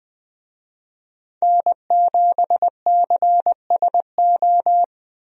Morse code, heard as a keyed tone, about 21 wpm, 700 Hz. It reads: N7CSO
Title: USA Call Signs - 5 Characters - Repeated 3x - 20wpm